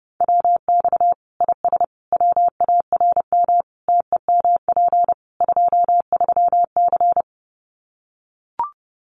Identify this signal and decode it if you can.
Morse code, heard as a keyed tone, about 30 wpm, 700 Hz. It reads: WX IS WARM TEMP 23C
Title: Common QSO Phrases 30wpm